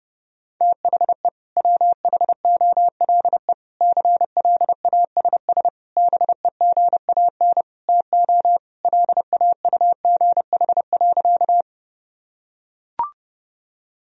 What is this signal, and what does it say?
Morse code, heard as a keyed tone, about 30 wpm, 700 Hz. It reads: THE WHOLE CLASS BEGAN TO LAUGH.
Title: Sentences from Top 1000 Words - Review Part 2 - 30wpm